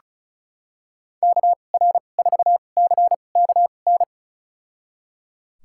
Morse code, heard as a keyed tone, about 36 wpm, spaced as for 25 wpm, 700 Hz. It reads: KR4CKN
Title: USA Call Signs - 2x Character Spacing - 35wpm